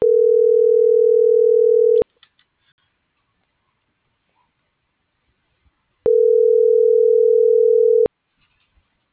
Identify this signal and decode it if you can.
Ringback tone (North America), 440 + 480 Hz.